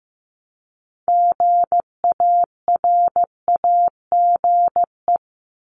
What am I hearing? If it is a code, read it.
Morse code, heard as a keyed tone, about 15 wpm, 700 Hz. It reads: GARAGE
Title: Multi-letter Phonemes Repeated 3x 15wpm